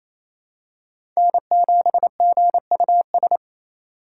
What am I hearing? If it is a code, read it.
Morse code, heard as a keyed tone, about 28 wpm, 700 Hz. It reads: N7GUS
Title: USA Call Signs - 2x Character Spacing - 28wpm